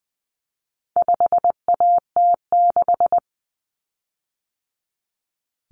Morse code, heard as a keyed tone, about 20 wpm, 700 Hz. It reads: 5AT6
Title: Common Cut Numbers - 4-Digit 20wpm